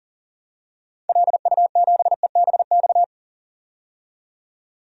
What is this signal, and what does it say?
Morse code, heard as a keyed tone, about 40 wpm, 700 Hz. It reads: LU7EBX
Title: DXCC-100 Call Signs - 6 Characters - 40wpm